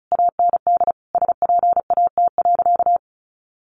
Morse code, heard as a keyed tone, about 34 wpm, 700 Hz. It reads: AND SPAT.
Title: A Sound Of Thunder 35wpm